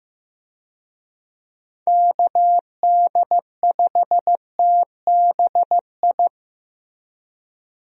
Morse code, heard as a keyed tone, about 15 wpm, 700 Hz. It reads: KD5TBI
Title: USA Call Signs - 6 Characters - Repeated 2x - 15wpm